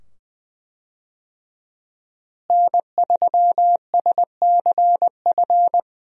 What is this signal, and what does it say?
Morse code, heard as a keyed tone, about 20 wpm, 700 Hz. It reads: N3SCF
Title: USA Call Signs - 5 Characters - Repeated 2x - 20wpm